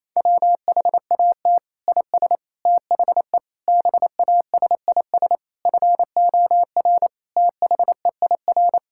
Morse code, heard as a keyed tone, about 28 wpm, 700 Hz. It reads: WHAT IS THE BASIS FOR THEIR
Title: Sentences from Top 1000 Words - Review Part 6 - 28wpm